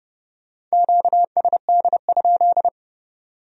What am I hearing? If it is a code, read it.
Morse code, heard as a keyed tone, about 38 wpm, 700 Hz. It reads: QSD?